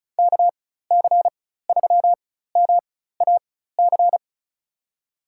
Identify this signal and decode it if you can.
Morse code, heard as a keyed tone, about 36 wpm, spaced as for 16 wpm, 700 Hz. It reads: KC3MAC
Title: USA Call Signs - 6 Characters - 4x Character Spacing - 35wpm